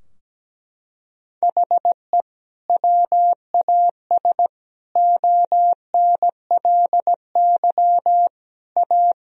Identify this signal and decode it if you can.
Morse code, heard as a keyed tone, about 17 wpm, 700 Hz. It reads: HE WAS ONLY A
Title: Sentences from Top 1000 Words - Review Part 4 - 17wpm